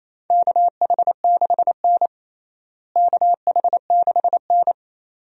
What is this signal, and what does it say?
Morse code, heard as a keyed tone, about 28 wpm, 700 Hz. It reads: KH6N KH6N
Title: DXCC-100 Call Signs - 4 Characters - Repeated 2x - 28wpm